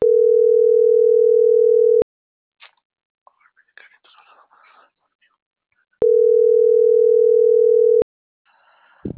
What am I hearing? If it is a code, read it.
Ringback tone (North America), 440 + 480 Hz.